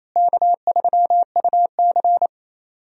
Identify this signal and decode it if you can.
Morse code, heard as a keyed tone, about 28 wpm, 700 Hz. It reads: K3UC
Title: USA Call Signs - 4 Characters - Repeated 2x - 28wpm